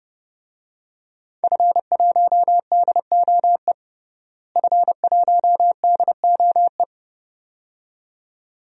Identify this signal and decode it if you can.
Morse code, heard as a keyed tone, about 29 wpm, 700 Hz. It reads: F1DOE F1DOE
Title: DXCC-100 Call Signs - 5 Characters - Repeated 2x - 30wpm